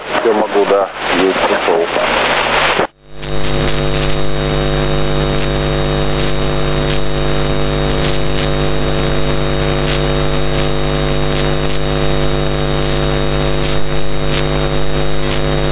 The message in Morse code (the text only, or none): none